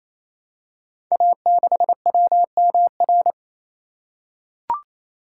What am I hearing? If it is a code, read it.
Morse code, heard as a keyed tone, about 28 wpm, 700 Hz. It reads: A6WMR
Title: Code Groups - 5 Letter:Numbers - 28wpm